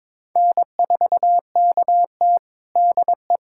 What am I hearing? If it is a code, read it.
Morse code, heard as a keyed tone, about 22 wpm, 700 Hz. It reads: N4KT DE
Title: USA Callsign DE Callsign - 4 Characters - 22wpm